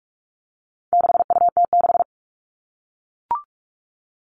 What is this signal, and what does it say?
Morse code, heard as a keyed tone, about 44 wpm, 700 Hz. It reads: 6UT6
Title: All Cut Numbers - 4-Digit 45wpm